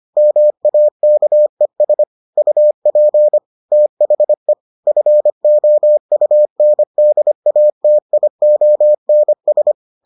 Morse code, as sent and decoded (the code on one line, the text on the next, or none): -- .- -.- . ... / ..- .--. / - .... / ..-. --- ..- -. -.. .- - .. --- -. ...
MAKES UP THE FOUNDATIONS